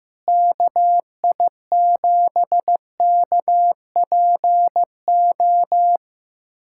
Morse code, heard as a keyed tone, about 16 wpm, 700 Hz. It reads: KI7KPO
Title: USA Callsign DE Callsign - 6 Characters - 15wpm